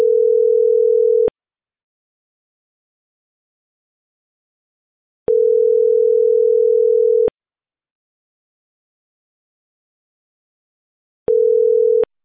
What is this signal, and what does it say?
Ringback tone (North America), 440 + 480 Hz.